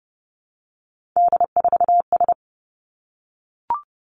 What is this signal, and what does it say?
Morse code, heard as a keyed tone, about 30 wpm, 700 Hz. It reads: D4S
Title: Three Challenging Letter-Number Combinations 30wpm